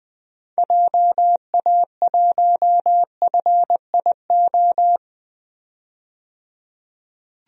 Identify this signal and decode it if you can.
Morse code, heard as a keyed tone, about 20 wpm, 700 Hz. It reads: JA1FIO
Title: DXCC-100 Call Signs - 6 Characters - 20wpm